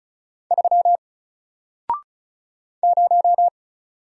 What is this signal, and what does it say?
Morse code, heard as a keyed tone, about 36 wpm, 700 Hz. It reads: 3 0
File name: Single-Letter-Number-35wpm